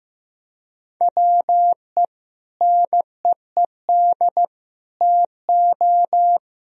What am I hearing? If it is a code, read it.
Morse code, heard as a keyed tone, about 15 wpm, 700 Hz. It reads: WE NEED TO
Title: Sentences from Top 1000 Words - Review Part 3 - 15wpm